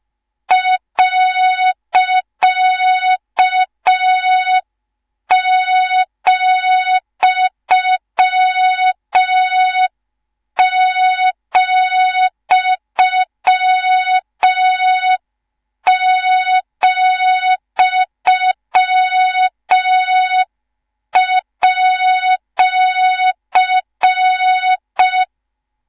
Morse code, heard as a keyed tone, about 5 wpm, 3700 Hz, with noise. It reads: .,,,@